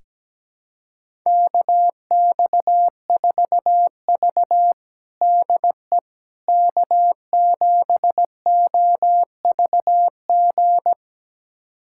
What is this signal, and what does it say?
Morse code, heard as a keyed tone, about 17 wpm, 700 Hz. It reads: KX4V DE K7OVG
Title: USA Callsign DE Callsign 17wpm